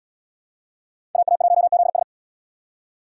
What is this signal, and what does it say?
Morse code, heard as a keyed tone, about 69 wpm, 700 Hz. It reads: AA1ZS